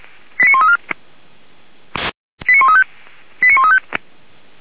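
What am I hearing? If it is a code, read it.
Five-tone selective call (ZVEI), heard as quick runs of five tones, tones of about 70 ms. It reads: 89134 89135 89135
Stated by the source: ZVEI five-tone call 89133 received by SDR on the 464 channel, feuerwehr-orlamuende.org fire brigade audio server.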